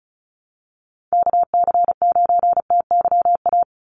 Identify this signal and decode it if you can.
Morse code, heard as a keyed tone, about 36 wpm, 700 Hz. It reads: KC9TYA
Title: NA SOTA Chaser Callsigns 35wpm